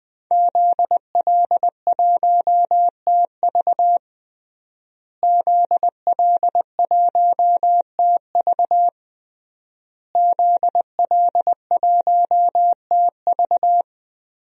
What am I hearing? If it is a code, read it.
Morse code, heard as a keyed tone, about 20 wpm, 700 Hz. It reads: ZL1TV ZL1TV ZL1TV
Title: DXCC-100 Call Signs - 5 Characters - Repeated 3x - 20wpm